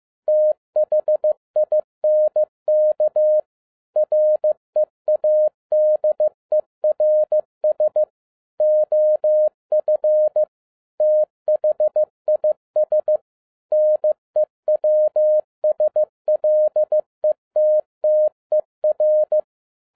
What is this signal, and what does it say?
Morse code, heard as a keyed tone, about 15 wpm, 610 Hz. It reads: THINK READERS OF THIS NEWSLETTER
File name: 2011-07-mp3 / n37 THE 3.11 2 15wpm